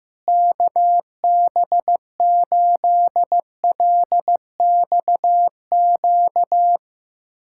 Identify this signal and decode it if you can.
Morse code, heard as a keyed tone, about 15 wpm, 700 Hz. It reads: KB8LXQ
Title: USA Callsign DE Callsign - 6 Characters - 15wpm